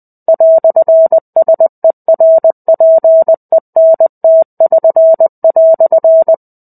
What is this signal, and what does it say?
Morse code, heard as a keyed tone, about 20 wpm, 650 Hz. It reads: "SERPENT<SN>"
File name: 05-Chapter-05 20 WPM